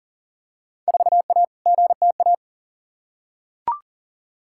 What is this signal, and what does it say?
Morse code, heard as a keyed tone, about 40 wpm, 700 Hz. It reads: 4A GTA
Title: Field Day Exchanges Encore - 40wpm